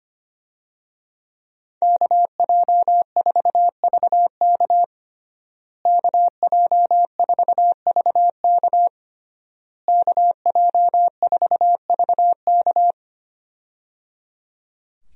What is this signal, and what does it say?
Morse code, heard as a keyed tone, about 25 wpm, 700 Hz. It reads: KJ4VK KJ4VK KJ4VK
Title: USA Call Signs - 5 Characters - Repeated 3x - 25wpm